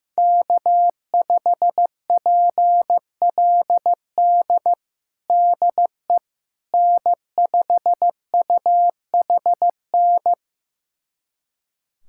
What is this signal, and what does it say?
Morse code, heard as a keyed tone, about 16 wpm, 700 Hz. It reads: K5PLD DE N5UHN
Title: USA Callsign DE Callsign - 5 Characters - 15wpm